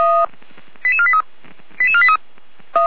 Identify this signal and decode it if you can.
Five-tone selective call (ZVEI), heard as quick runs of five tones, tones of about 70 ms. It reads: 88483 88483
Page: /455/SDR/ZVEI/